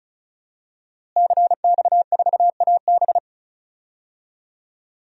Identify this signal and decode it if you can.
Morse code, heard as a keyed tone, about 36 wpm, 700 Hz. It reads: CX4AB
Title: DXCC-100 Call Signs - 5 Characters - Repeated 3x - 35wpm